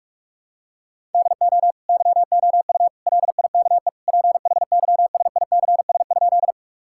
Morse code, heard as a keyed tone, about 46 wpm, 700 Hz. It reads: DO YOU LIKE PHYSICS?
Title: Sentences from Top 1000 Words - Review Part 5 - 45wpm